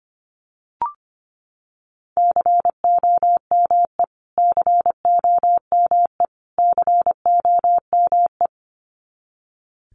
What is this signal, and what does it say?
Morse code, heard as a keyed tone, about 25 wpm, 700 Hz. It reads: COME COME COME
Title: Top 300 Words Repeated 25wpm